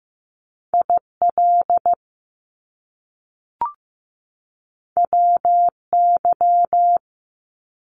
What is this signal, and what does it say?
Morse code, heard as a keyed tone, about 15 wpm, 700 Hz. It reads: IL WY
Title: US States 15wpm